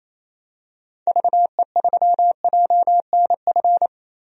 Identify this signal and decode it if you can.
Morse code, heard as a keyed tone, about 28 wpm, 700 Hz. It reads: VE3JNF
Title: DXCC-100 Call Signs - 6 Characters - 28wpm